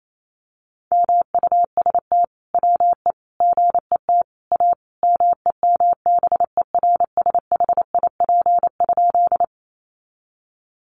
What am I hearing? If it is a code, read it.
Morse code, heard as a keyed tone, about 28 wpm, 700 Hz. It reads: MUST WE GET A MEMBERSHIP?
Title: Sentences from Top 1400 Words 28wpm